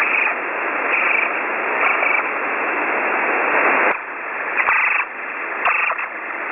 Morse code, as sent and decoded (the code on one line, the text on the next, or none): none